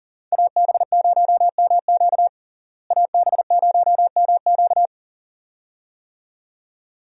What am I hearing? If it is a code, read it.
Morse code, heard as a keyed tone, about 40 wpm, 700 Hz. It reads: AB0MQ AB0MQ
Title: USA Call Signs - 5 Characters - Repeated 2x - 40wpm